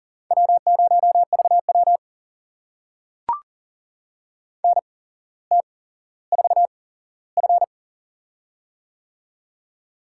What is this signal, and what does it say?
Morse code, heard as a keyed tone, about 40 wpm, spaced as for 17 wpm, 700 Hz. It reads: W0VW NT4F
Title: USA Call Signs - 4 Characters - 8x Character Spacing - 40wpm